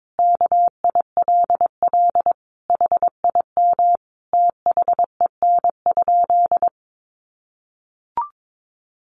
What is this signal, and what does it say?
Morse code, heard as a keyed tone, about 22 wpm, 700 Hz. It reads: KILL HIM THEN?
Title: Sentences from Top 1400 Words 22wpm